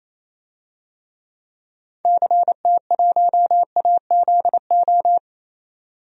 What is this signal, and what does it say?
Morse code, heard as a keyed tone, about 28 wpm, 700 Hz. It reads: CT1AZO
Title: DXCC-100 Call Signs - 6 Characters - 28wpm